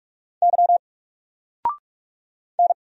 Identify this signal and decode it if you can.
Morse code, heard as a keyed tone, about 44 wpm, 700 Hz. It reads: Y N
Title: Single Letter-Number Alt 45wpm